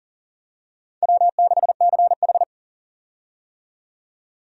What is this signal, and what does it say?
Morse code, heard as a keyed tone, about 40 wpm, 700 Hz. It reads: W6CH
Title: USA Call Signs - 4 Characters - 40wpm